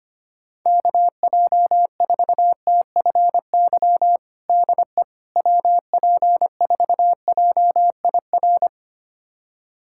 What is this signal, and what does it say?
Morse code, heard as a keyed tone, about 25 wpm, 700 Hz. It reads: KJ4TFY DE WP4JIR
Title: USA Callsign DE Callsign 25wpm